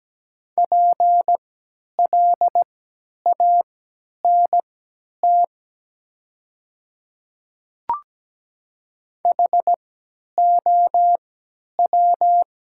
Morse code, heard as a keyed tone, about 17 wpm, spaced as for 10 wpm, 700 Hz. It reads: PLANT HOW
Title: 3x Character Spacing - Top 300 Words - 17wpm